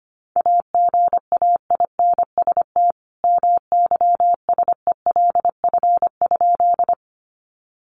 Morse code, heard as a keyed tone, about 25 wpm, 700 Hz. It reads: AGAINST MYSELF?
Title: Sentences from Top 1300 Words 25wpm